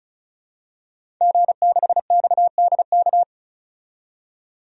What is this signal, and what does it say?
Morse code, heard as a keyed tone, about 36 wpm, 700 Hz. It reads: G6XDK
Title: DXCC-100 Call Signs - 5 Characters - 35wpm